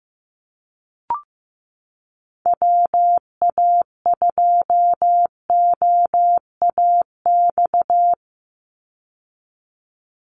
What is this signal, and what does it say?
Morse code, heard as a keyed tone, about 15 wpm, 700 Hz. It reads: WA2OAX
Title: Call Signs Repeated 3x - 15wpm